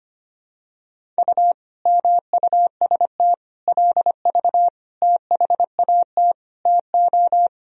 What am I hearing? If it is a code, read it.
Morse code, heard as a keyed tone, about 25 wpm, 700 Hz. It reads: U MUST LV THAT TO
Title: Sentences from Top 400 Words and Abbreviations 25wpm